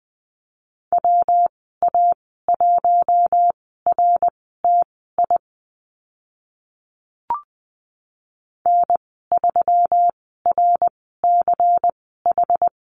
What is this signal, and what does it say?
Morse code, heard as a keyed tone, about 20 wpm, spaced as for 14 wpm, 700 Hz. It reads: WA1RTI N3RCH
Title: 122 Lesson 20g - Number 3 - Call Signs 20wpm